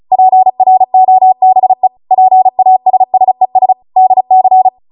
Morse code, heard as a keyed tone, about 36 wpm, 750 Hz. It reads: PROBE PASSES DC